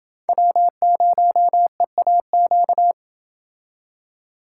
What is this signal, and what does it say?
Morse code, heard as a keyed tone, about 29 wpm, 700 Hz. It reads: W0EAQ